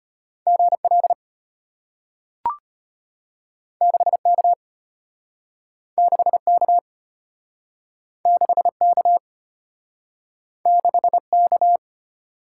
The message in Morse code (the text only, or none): GL 6K 6K 6K 6K